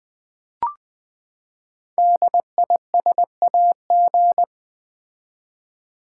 Morse code, heard as a keyed tone, about 20 wpm, 700 Hz. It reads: DISAG
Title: Build A Word - 5 Letters At A Time - 12-Letter Words - 20wpm